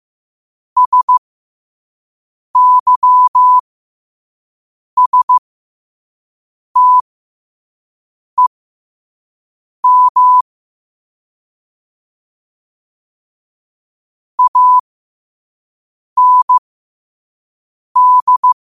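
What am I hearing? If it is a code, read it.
Morse code, heard as a keyed tone, about 15 wpm, spaced as for 5 wpm, 1000 Hz. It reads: SYSTEM AND
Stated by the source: Morse code sent at 5 wpm for learning.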